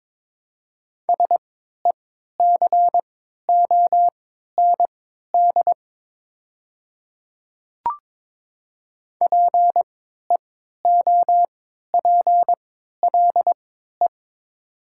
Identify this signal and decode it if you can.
Morse code, heard as a keyed tone, about 22 wpm, spaced as for 13 wpm, 700 Hz. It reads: SECOND PEOPLE
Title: 3x Character Spacing - Top 300 Words - 22wpm